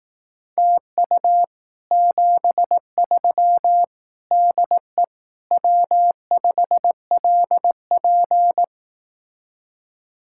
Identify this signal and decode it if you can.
Morse code, heard as a keyed tone, about 18 wpm, 700 Hz. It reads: TU 73 DE W5LP
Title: Speed-Racing Common QSO Phrases Encore - 15wpm